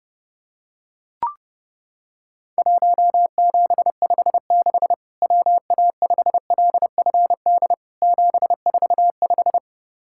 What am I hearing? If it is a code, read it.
Morse code, heard as a keyed tone, about 30 wpm, 700 Hz. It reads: 1756 WA5LFD 745
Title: Worked All Europe DX Contest - QTC Exchange - 30wpm